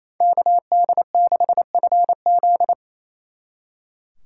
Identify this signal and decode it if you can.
Morse code, heard as a keyed tone, about 34 wpm, 700 Hz. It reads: KD6FZ
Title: Speed Racing - USA 2x2 Call Signs 28wpm